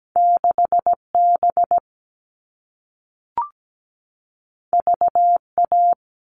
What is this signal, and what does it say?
Morse code, heard as a keyed tone, about 17 wpm, 700 Hz. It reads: 6B VA
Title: All Cut Numbers - 2-Digit 17wpm